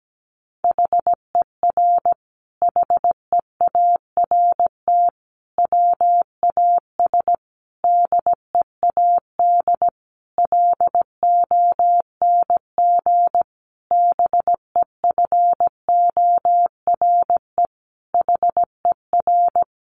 Morse code, heard as a keyed tone, about 17 wpm, 700 Hz. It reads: HER HEART WAS DEAD LONG BEFORE HER
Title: Sentences from Top 1500 Words 17wpm